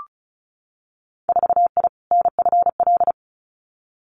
Morse code, heard as a keyed tone, about 36 wpm, 700 Hz. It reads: 4I NFL
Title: Winter Field Day Exchanges Encore 35wpm